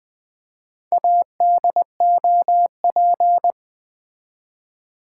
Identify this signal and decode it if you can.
Morse code, heard as a keyed tone, about 20 wpm, 700 Hz. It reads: ADOP
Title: Build A Word - 2 Letters At A Time - 8-Letter Words - 20wpm